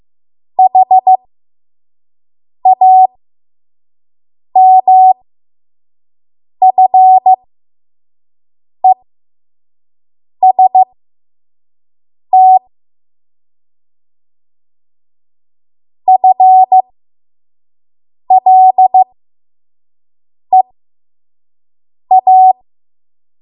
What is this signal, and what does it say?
Morse code, heard as a keyed tone, about 15 wpm, spaced as for 5 wpm, 750 Hz. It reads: HAMFEST FLEA
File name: Morse / 05WPM